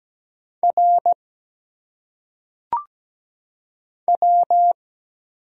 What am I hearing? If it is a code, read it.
Morse code, heard as a keyed tone, about 17 wpm, 700 Hz. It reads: R W
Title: Single Letters - Phonetic Pronunciation 17wpm